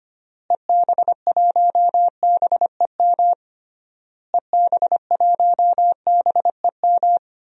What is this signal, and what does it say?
Morse code, heard as a keyed tone, about 25 wpm, 700 Hz. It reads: EB1BEM EB1BEM
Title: DXCC-100 Call Signs - 6 Characters - Repeated 3x - 25wpm